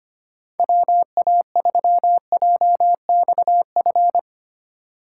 Morse code, heard as a keyed tone, about 25 wpm, 700 Hz. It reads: WA3JXF
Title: USA Call Signs - 3x Character Spacing - 25wpm